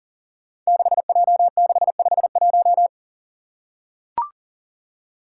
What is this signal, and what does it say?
Morse code, heard as a keyed tone, about 37 wpm, 700 Hz. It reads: 6J651
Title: Five Challenging Letter-Number Combinations 40wpm